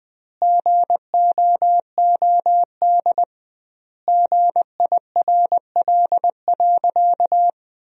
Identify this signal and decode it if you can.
Morse code, heard as a keyed tone, about 20 wpm, 700 Hz. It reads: GOOD GIRL.
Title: 2x Word-Spacing - Sentences from Top 400 Words - 20wpm